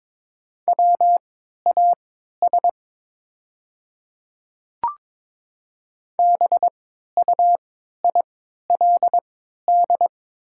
Morse code, heard as a keyed tone, about 22 wpm, spaced as for 12 wpm, 690 Hz. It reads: WAS BUILD
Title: 3x Character Spacing - Top 300 Words - 22wpm